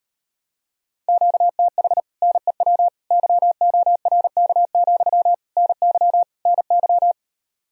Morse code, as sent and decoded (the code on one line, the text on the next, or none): --.- - .... / -. .-- / -.-- --- .-. -.- --..-- / -. -.-- / -. -.--
QTH NEW YORK, NY NY